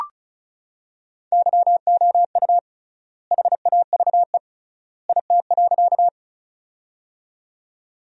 Morse code, heard as a keyed tone, about 36 wpm, 700 Hz. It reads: YOU HAVE IT.
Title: 3x Word-Spacing - Sentences from Top 100 Words - 35wpm